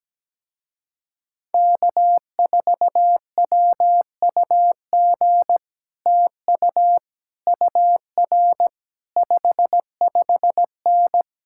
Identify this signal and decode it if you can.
Morse code, heard as a keyed tone, about 17 wpm, 700 Hz. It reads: K4WUG TU UR 55N
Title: Speed-Racing Common QSO Phrases - 17wpm